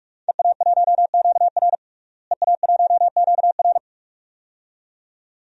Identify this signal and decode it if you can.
Morse code, heard as a keyed tone, about 41 wpm, 700 Hz. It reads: EA1QR EA1QR
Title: DXCC-100 Call Signs - 5 Characters - Repeated 2x - 45wpm